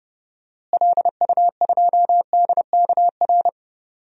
Morse code, heard as a keyed tone, about 32 wpm, 700 Hz. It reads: LU2DKR
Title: DXCC-100 Call Signs - 6 Characters - Repeated 2x - 30wpm